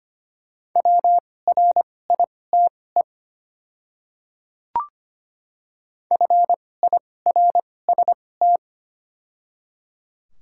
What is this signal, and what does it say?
Morse code, heard as a keyed tone, about 25 wpm, spaced as for 18 wpm, 700 Hz. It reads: WRITE FIRST
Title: 2x Character Spacing - Top 100 Words - 25wpm